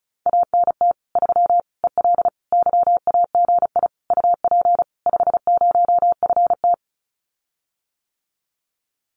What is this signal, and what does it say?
Morse code, heard as a keyed tone, about 34 wpm, 700 Hz. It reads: ANT 3 EL YAGI UP 50FT
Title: Common QSO Phrases Encore 35wpm